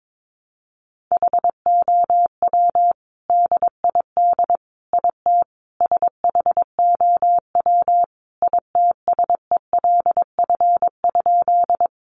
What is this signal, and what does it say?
Morse code, heard as a keyed tone, about 22 wpm, 700 Hz. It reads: HOW DID IT SHOW ITSELF?
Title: Sentences from Top 800 Words 22wpm